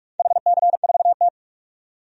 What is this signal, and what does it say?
Morse code, heard as a keyed tone, about 46 wpm, 700 Hz. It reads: HC4T
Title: DXCC-100 Call Signs - 4 Characters - 45wpm